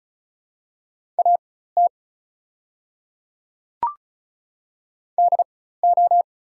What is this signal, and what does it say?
Morse code, heard as a keyed tone, about 36 wpm, spaced as for 16 wpm, 700 Hz. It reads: AT DO
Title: 4x Character Spacing - Top 100 Words - 35wpm